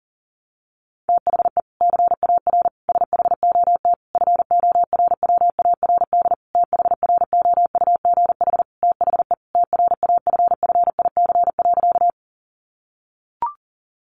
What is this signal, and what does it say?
Morse code, heard as a keyed tone, about 40 wpm, 700 Hz. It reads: THE CAR SHOT FORWARD THROUGH THE TRAFFIC.
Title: Sentences from Top 1300 Words 40wpm